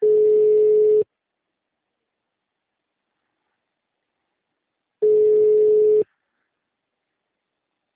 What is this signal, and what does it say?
Ringback tone (Europe, 425 Hz).